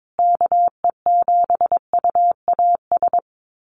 Morse code, heard as a keyed tone, about 22 wpm, 700 Hz. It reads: KE7UAS
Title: USA Call Signs Repeated 3x 22wpm